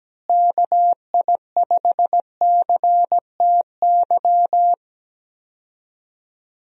Morse code, heard as a keyed tone, about 17 wpm, 700 Hz. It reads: KI5CTY
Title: USA Call Signs - 6 Characters - Repeated 3x - 17wpm